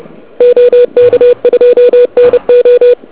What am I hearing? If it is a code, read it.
Morse code, heard as a keyed tone, about 29 wpm, 2450 Hz, with noise. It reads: OK2NO
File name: ARCHIV / KX3 / KV PA